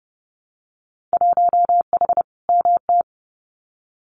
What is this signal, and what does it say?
Morse code, heard as a keyed tone, about 30 wpm, 700 Hz. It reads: 1H MT
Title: Winter Field Day Exchanges Encore 30wpm